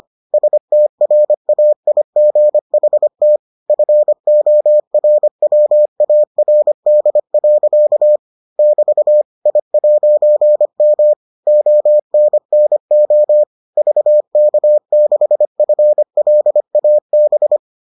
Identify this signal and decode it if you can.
Morse code, heard as a keyed tone, about 25 wpm, 600 Hz. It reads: STRAIGHT FORWARD. <BT> I'M ONNO VK6FLAB